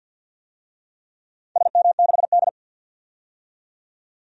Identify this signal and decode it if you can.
Morse code, heard as a keyed tone, about 47 wpm, 700 Hz. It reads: SM6D
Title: DXCC-100 Call Signs - 4 Characters - 50wpm